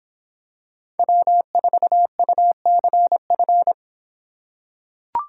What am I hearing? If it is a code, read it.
Morse code, heard as a keyed tone, about 26 wpm, 700 Hz. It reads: W4UCF